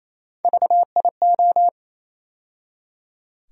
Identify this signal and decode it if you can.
Morse code, heard as a keyed tone, about 28 wpm, 700 Hz. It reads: VIO
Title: USA Call Sign Postfixes - 3 Letters 28wpm